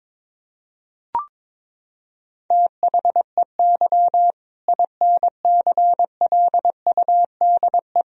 Morse code, heard as a keyed tone, about 22 wpm, 700 Hz. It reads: THEY INCLUDE
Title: Sentences from Top 1000 Words - Review Part 3 - 22wpm